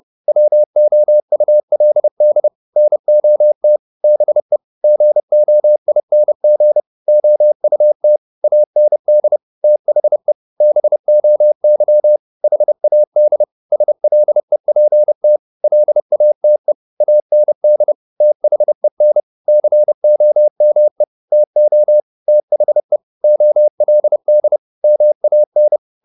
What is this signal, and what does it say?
Morse code, heard as a keyed tone, about 30 wpm, 600 Hz. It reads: WOULD NOT BE GOING OUT AND THE BOY HAD SLEPT LATE AND THEN COME TO THE OLD MAN